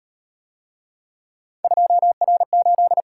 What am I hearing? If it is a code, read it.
Morse code, heard as a keyed tone, about 38 wpm, 700 Hz. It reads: 2R8